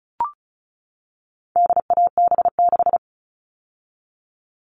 Morse code, heard as a keyed tone, about 34 wpm, 700 Hz. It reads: DAB6
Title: All Cut Numbers - 4-Digit 35wpm